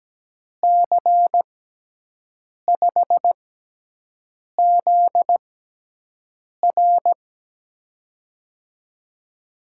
Morse code, heard as a keyed tone, about 17 wpm, spaced as for 6 wpm, 700 Hz. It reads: C5ZR
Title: USA Call Signs - 6x Character Spacing - 17wpm